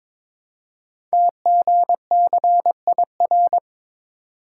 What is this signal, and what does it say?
Morse code, heard as a keyed tone, about 22 wpm, 700 Hz. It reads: TGCIR
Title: Code Groups - 5 Letters - 22wpm